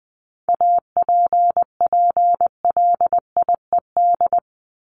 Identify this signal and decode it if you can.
Morse code, heard as a keyed tone, about 20 wpm, 700 Hz. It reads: APPLIED
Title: ED Suffix - Top 100 Words with Base Words 20wpm